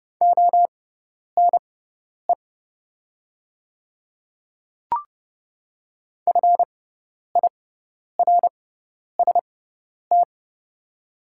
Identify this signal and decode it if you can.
Morse code, heard as a keyed tone, about 30 wpm, spaced as for 10 wpm, 700 Hz. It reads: ONE FIRST
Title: 6x Character Spacing - Top 100 Words - 30wpm